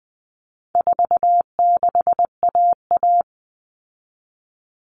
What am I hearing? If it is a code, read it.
Morse code, heard as a keyed tone, about 20 wpm, 700 Hz. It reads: 46AA
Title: All Cut Numbers - 4-Digit 20wpm